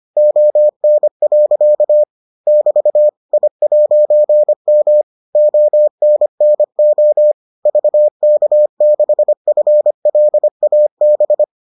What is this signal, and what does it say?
Morse code, heard as a keyed tone, about 25 wpm, 600 Hz. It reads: ON. <BT> I'M ONNO VK6FLAB